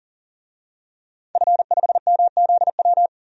Morse code, heard as a keyed tone, about 40 wpm, 700 Hz. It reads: F5MZW
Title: DXCC-100 Call Signs - 5 Characters - 40wpm